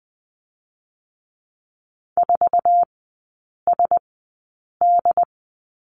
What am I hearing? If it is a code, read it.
Morse code, heard as a keyed tone, about 20 wpm, 700 Hz. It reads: 4 S D
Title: 100 Lesson 17e - Letter u - Three Characters 20wpm